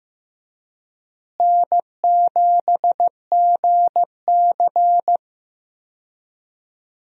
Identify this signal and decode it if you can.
Morse code, heard as a keyed tone, about 15 wpm, 700 Hz. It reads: N7GC
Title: USA Call Signs - 4 Characters - Repeated 2x - 15wpm